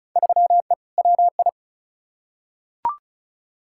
Morse code, heard as a keyed tone, about 36 wpm, 690 Hz. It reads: 3E WI
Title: Field Day Exchanges Encore - 35wpm